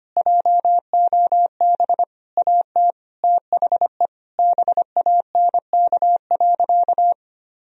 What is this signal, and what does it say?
Morse code, heard as a keyed tone, about 25 wpm, 700 Hz. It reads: JOB AT THE BANK.
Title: Sentences from Top 1000 Words - Review Part 4 - 25wpm